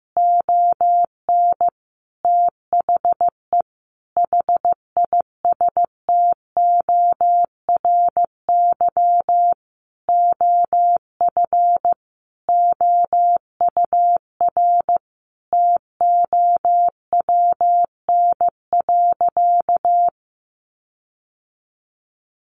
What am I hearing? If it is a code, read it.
Morse code, heard as a keyed tone, about 15 wpm, 700 Hz. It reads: ON THE HISTORY OF OUR TOWN.
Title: Sentences from Top 1500 Words 15wpm